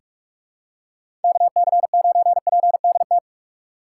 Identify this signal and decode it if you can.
Morse code, heard as a keyed tone, about 46 wpm, 700 Hz. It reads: KC9PDT
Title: USA Call Signs - 6 Characters - 2x Character Spacing - 45wpm